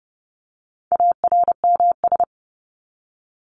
Morse code, heard as a keyed tone, about 30 wpm, 700 Hz. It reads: ARMS
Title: ES:S Suffix - Top 100 Words with Base Words 30wpm